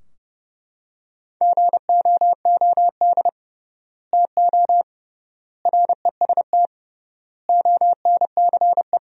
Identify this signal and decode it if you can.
Morse code, heard as a keyed tone, about 30 wpm, 700 Hz. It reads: GOOD TO REST ONCE
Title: 3x Word-Spacing - Sentences from Top 500 Words - 30wpm